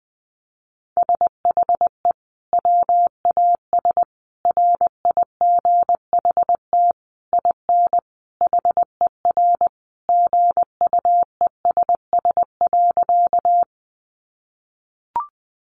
Morse code, heard as a keyed tone, about 20 wpm, 700 Hz. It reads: SHE WAS RIGHT IN HER GUESS.
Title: Sentences from Top 1500 Words 20wpm